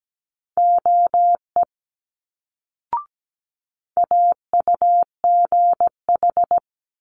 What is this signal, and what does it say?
Morse code, heard as a keyed tone, about 17 wpm, 700 Hz. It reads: OE AUGH
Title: Multi-letter Phonemes Quick Fire 17wpm